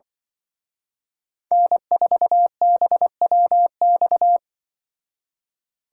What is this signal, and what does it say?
Morse code, heard as a keyed tone, about 24 wpm, 700 Hz. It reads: N4BWX
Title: Speed Racing - USA Call Signs 20wpm